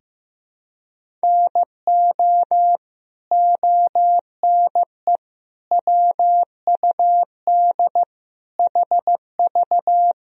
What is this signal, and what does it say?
Morse code, heard as a keyed tone, about 15 wpm, 700 Hz. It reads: NO ONE WUD HV
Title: Sentences from Top 300 Words and Abbreviations 15wpm